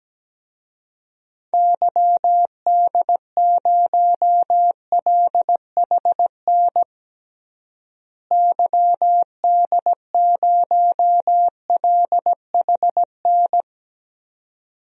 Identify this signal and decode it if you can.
Morse code, heard as a keyed tone, about 17 wpm, 700 Hz. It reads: YD0LHN YD0LHN
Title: DXCC-100 Call Signs - 6 Characters - Repeated 2x - 17wpm